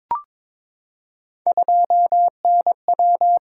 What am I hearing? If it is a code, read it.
Morse code, heard as a keyed tone, about 22 wpm, 700 Hz. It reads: 2NW
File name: practice / 22wpm / Three Letter-Number Combinations - 22wpm